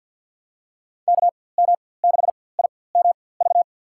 Morse code, heard as a keyed tone, about 52 wpm, spaced as for 23 wpm, 700 Hz. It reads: KM6IMV
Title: USA Call Signs - 6 Characters - 4x Character Spacing - 50wpm